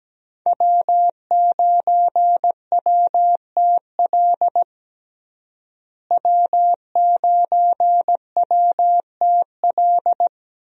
Morse code, heard as a keyed tone, about 17 wpm, 700 Hz. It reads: W9WTL W9WTL
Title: USA Call Signs - 5 Characters - Repeated 2x - 17wpm